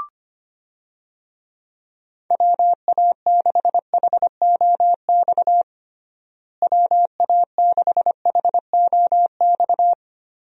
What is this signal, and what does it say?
Morse code, heard as a keyed tone, about 25 wpm, 700 Hz. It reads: WA6HOX WA6HOX
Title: USA Call Signs - 6 Characters - Repeated 2x - 25wpm